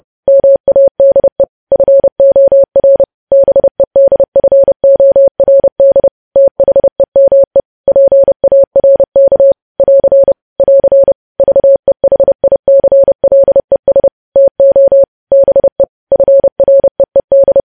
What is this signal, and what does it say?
Morse code, heard as a keyed tone, about 30 wpm, 550 Hz. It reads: MADE FOR BEDFORD THEME PARK <AR> <AR> VEHICLES TO BE FREED
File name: podcast / News headlines-1x-30x30